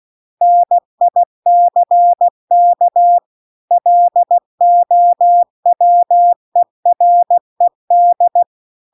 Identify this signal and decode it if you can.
Morse code, heard as a keyed tone, about 16 wpm, 700 Hz. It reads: NICK LOWERED